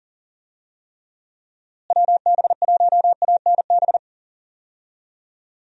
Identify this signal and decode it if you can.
Morse code, heard as a keyed tone, about 40 wpm, 700 Hz. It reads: WB1ANB
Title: USA Call Signs - 6 Characters - Repeated 3x - 40wpm